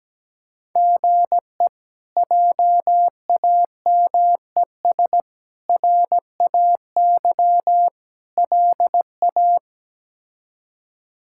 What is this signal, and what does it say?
Morse code, heard as a keyed tone, about 17 wpm, 700 Hz. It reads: GE JAMES RAY LA
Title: K1USN SST QSO Practice - 17wpm